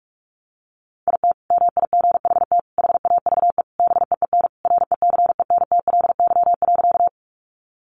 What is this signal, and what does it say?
Morse code, heard as a keyed tone, about 44 wpm, 700 Hz. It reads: IT MIGHT HAVE BEEN RECENTLY.
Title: Sentences from Top 1200 Words 45wpm